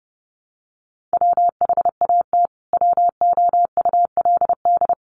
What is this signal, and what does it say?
Morse code, heard as a keyed tone, about 30 wpm, 700 Hz. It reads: WHAT WOULD
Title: Sentences from Top 300 Words 30wpm